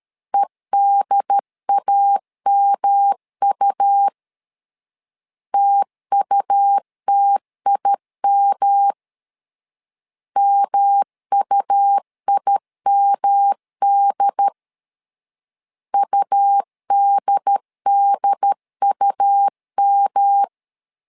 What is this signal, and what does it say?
Morse code, heard as a keyed tone, about 13 wpm, 800 Hz. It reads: EDAMU TUTIM MUIMD UDDUM
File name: LEZIONI / 3 MWRO / 3 M 12wpm 1541